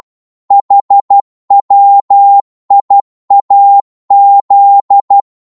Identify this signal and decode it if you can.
Morse code, heard as a keyed tone, about 12 wpm, 800 Hz. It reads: HWIAZ
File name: Morse / ltr-12wpm